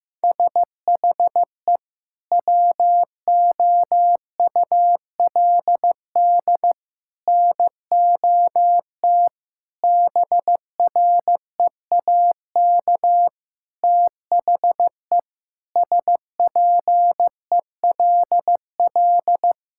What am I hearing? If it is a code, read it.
Morse code, heard as a keyed tone, about 15 wpm, 700 Hz. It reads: SHE WOULD NOT BREAK THE SPELL
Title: Sentences from Top 1000 Words - Review Part 6 - 15wpm